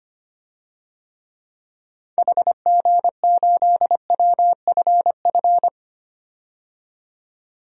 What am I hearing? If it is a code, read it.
Morse code, heard as a keyed tone, about 25 wpm, 700 Hz. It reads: HG8WFF
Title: DXCC-100 Call Signs - 6 Characters - 25wpm